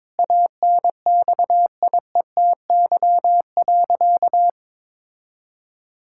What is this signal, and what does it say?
Morse code, heard as a keyed tone, about 22 wpm, 700 Hz. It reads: ANXIETY.
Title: Sentences from Top 1600 Words 22wpm